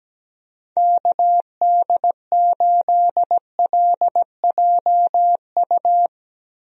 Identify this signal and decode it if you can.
Morse code, heard as a keyed tone, about 17 wpm, 700 Hz. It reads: KD8LJU
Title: USA Call Signs - 6 Characters - 17wpm